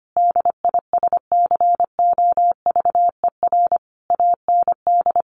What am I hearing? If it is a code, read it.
Morse code, heard as a keyed tone, about 25 wpm, 700 Hz. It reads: DISCOVER AND
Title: Sentences from Top 1400 Words 25wpm